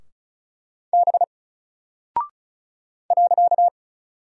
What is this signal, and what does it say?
Morse code, heard as a keyed tone, about 35 wpm, 700 Hz. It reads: B .
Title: Single Character (Letter:Number:Punctuation:Prosign) 35wpm